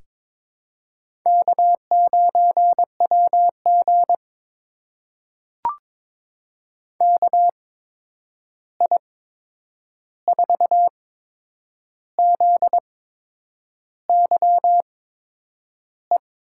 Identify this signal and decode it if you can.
Morse code, heard as a keyed tone, about 22 wpm, spaced as for 8 wpm, 700 Hz. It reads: K9WG KI4ZYE
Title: USA Call Signs - 8x Character Spacing - 22wpm